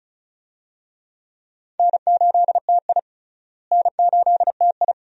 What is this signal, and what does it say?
Morse code, heard as a keyed tone, about 36 wpm, 700 Hz. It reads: N8TI N8TI
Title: USA Call Signs - 4 Characters - Repeated 2x - 35wpm